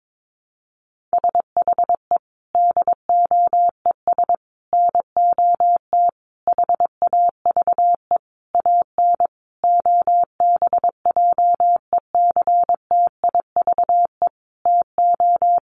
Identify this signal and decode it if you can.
Morse code, heard as a keyed tone, about 22 wpm, 700 Hz. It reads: SHE DOES NOT HAVE AN OBJECTIVE TO
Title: Sentences from Top 1200 Words 22wpm